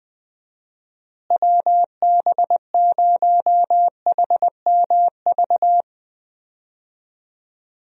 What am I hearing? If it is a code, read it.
Morse code, heard as a keyed tone, about 20 wpm, 700 Hz. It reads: WB0HMV
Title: USA Call Signs - 6 Characters - Repeated 3x - 20wpm